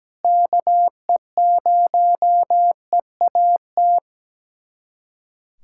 Morse code, heard as a keyed tone, about 17 wpm, 700 Hz. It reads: KE0EAT
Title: USA Callsign DE Callsign 17wpm